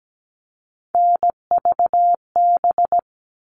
Morse code, heard as a keyed tone, about 17 wpm, 700 Hz. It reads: NVB
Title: All Cut Numbers - 3-Digit 17wpm